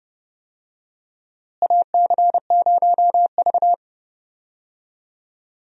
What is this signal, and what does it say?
Morse code, heard as a keyed tone, about 32 wpm, 700 Hz. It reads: AC0V
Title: USA Call Signs - 4 Characters - Repeated 3x - 30wpm